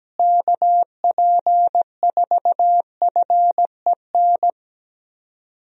Morse code, heard as a keyed tone, about 17 wpm, 700 Hz. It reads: KP4FEN
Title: DXCC-100 Call Signs - 6 Characters - Repeated 2x - 17wpm